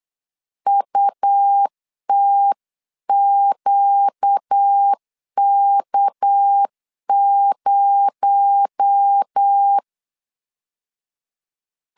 Morse code, heard as a keyed tone, about 8 wpm, 800 Hz. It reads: UTQK0
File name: 8wpm